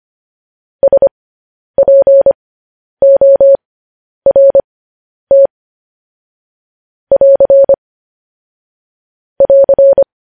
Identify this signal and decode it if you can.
Morse code, heard as a keyed tone, about 25 wpm, spaced as for 10 wpm, 550 Hz. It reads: SPORT <AR> <AR>
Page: News Headlines in Morse Code at 10 WPM